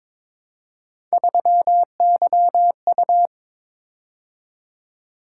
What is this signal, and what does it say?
Morse code, heard as a keyed tone, about 22 wpm, 700 Hz. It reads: 3YU
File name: practice / 22wpm / Three Letter-Number Combinations - 22wpm